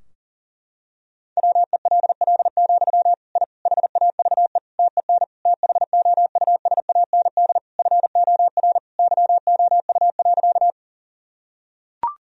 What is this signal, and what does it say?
Morse code, heard as a keyed tone, about 37 wpm, 700 Hz. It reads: WELL, I HAVE TEN THOUSAND FOR YOU.
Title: Sentences from Top 1000 Words - Review Part 3 - 40wpm